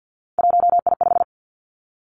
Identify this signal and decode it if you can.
Morse code, heard as a keyed tone, about 50 wpm, 700 Hz. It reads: 1I5
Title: Three Challenging Letter-Number Combinations 50wpm